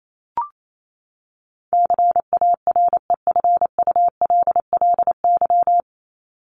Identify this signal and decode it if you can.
Morse code, heard as a keyed tone, about 28 wpm, 700 Hz. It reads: CAREFULLY
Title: LY Suffix - Top 100 Words with Base Words 28wpm